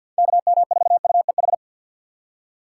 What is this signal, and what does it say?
Morse code, heard as a keyed tone, about 46 wpm, 700 Hz. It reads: KM4UEH
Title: USA Call Signs - 6 Characters - 3x Character Spacing - 50wpm